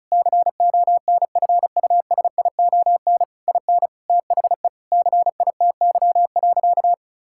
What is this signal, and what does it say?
Morse code, heard as a keyed tone, about 36 wpm, 700 Hz. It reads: CONFUSION IN THE CITY.
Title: Sentences from Top 1600 Words 35wpm